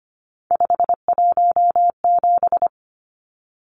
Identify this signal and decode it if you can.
Morse code, heard as a keyed tone, about 25 wpm, 700 Hz. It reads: 517
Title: Numbers - 3-Digit 25wpm